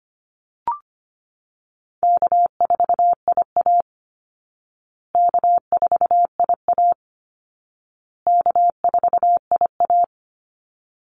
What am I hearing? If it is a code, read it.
Morse code, heard as a keyed tone, about 25 wpm, 700 Hz. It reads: K4IA K4IA K4IA
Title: Easy Call Signs Repeated 3x - 25wpm